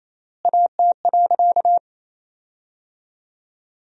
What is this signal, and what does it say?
Morse code, heard as a keyed tone, about 27 wpm, 700 Hz. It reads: AT.
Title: 3x Word-Spacing - Sentences from Top 200 Words - 28wpm